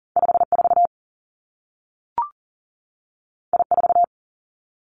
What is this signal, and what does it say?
Morse code, heard as a keyed tone, about 41 wpm, 700 Hz. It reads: 54 I4
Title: Two Challenging Letter-Number Combinations 40wpm